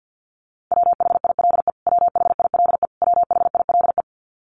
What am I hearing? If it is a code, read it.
Morse code, heard as a keyed tone, about 50 wpm, 700 Hz. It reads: WHILE WHILE WHILE
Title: Top 300 Words Repeated 50wpm